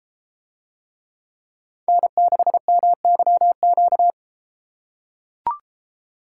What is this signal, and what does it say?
Morse code, heard as a keyed tone, about 33 wpm, 700 Hz. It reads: N6MYQ